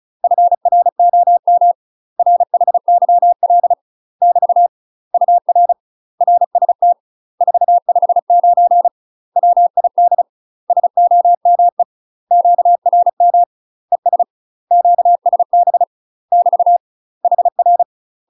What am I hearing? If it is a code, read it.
Morse code, heard as a keyed tone, about 34 wpm, 700 Hz. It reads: FROM RHYL <BT> UR RST 459 WID SOME QRM ES QSB <BT> HR
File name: UK MASTER QSOs TRAINING / 35WPM / UK MASTER QSOs 0004